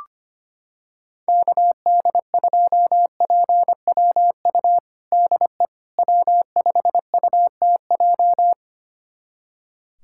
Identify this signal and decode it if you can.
Morse code, heard as a keyed tone, about 25 wpm, 700 Hz. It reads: KD2PWU DE W5UTJ
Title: USA Callsign DE Callsign 25wpm